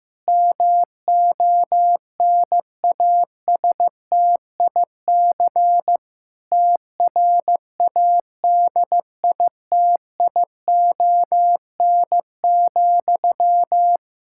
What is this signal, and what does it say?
Morse code, heard as a keyed tone, about 15 wpm, 700 Hz. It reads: MONASTIC TRADITION,
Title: Pangram Challenge v2 - 15wpm